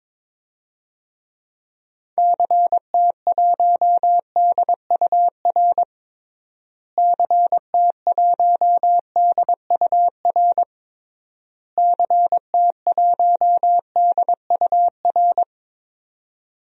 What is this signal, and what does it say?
Morse code, heard as a keyed tone, about 22 wpm, 700 Hz. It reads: CT1DUR CT1DUR CT1DUR
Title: DXCC-100 Call Signs - 6 Characters - Repeated 3x - 22wpm